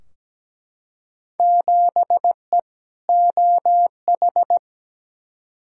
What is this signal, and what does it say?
Morse code, heard as a keyed tone, about 17 wpm, 700 Hz. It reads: 7E OH
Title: Field Day Exchanges Encore - 17wpm